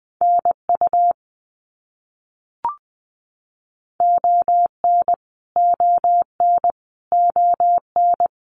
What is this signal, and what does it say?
Morse code, heard as a keyed tone, about 20 wpm, 700 Hz. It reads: NU ON ON ON
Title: Canadian Provinces and Territories - Repeated 3x - 20wpm